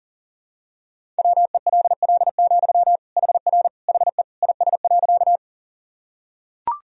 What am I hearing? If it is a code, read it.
Morse code, heard as a keyed tone, about 40 wpm, 700 Hz. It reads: WELL, HR HE IS.
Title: Sentences from Top 200 Words and Abbreviations 40wpm